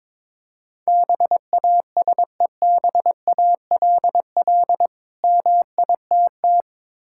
Morse code, heard as a keyed tone, about 22 wpm, 690 Hz. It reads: BASEBALL MITT
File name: Difficult Words with Hints - 22wpm